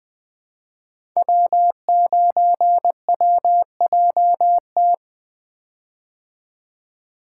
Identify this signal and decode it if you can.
Morse code, heard as a keyed tone, about 21 wpm, 700 Hz. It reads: W9WJT
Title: Code Groups - 5 Letter:Numbers - 20wpm